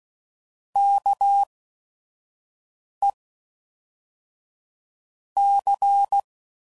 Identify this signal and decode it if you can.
Morse code, heard as a keyed tone, about 16 wpm, spaced as for 5 wpm, 800 Hz. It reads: KE C
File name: CW Code Groups @5WPM / QSOs / QSO WD0FBT